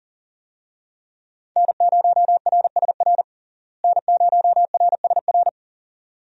Morse code, heard as a keyed tone, about 40 wpm, 700 Hz. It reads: N0RSR N0RSR
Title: USA Call Signs - 5 Characters - Repeated 2x - 40wpm